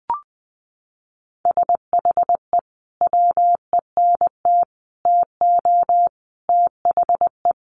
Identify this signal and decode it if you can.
Morse code, heard as a keyed tone, about 20 wpm, 700 Hz. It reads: SHE WENT TO THE
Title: Sentences from Top 900 Words 20wpm